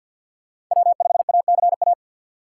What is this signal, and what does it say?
Morse code, heard as a keyed tone, about 47 wpm, 700 Hz. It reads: W5ACA
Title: USA Call Signs - 4x Character Spacing - 50wpm